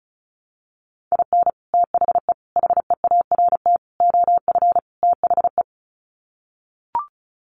Morse code, heard as a keyed tone, about 34 wpm, 700 Hz. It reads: IN THE HEART OF THE
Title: Sets of 5 Words 35wpm